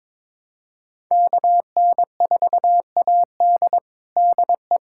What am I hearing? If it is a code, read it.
Morse code, heard as a keyed tone, about 22 wpm, 700 Hz. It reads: KN4AD DE
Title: USA Callsign DE Callsign - 5 Characters - 22wpm